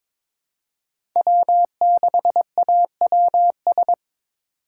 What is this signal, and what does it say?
Morse code, heard as a keyed tone, about 22 wpm, 700 Hz. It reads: W6AWS
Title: K1USN SST QSO Practice - 22wpm